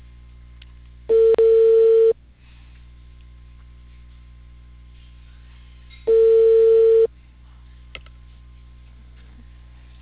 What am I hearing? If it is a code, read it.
Ringback tone (China, 450 Hz).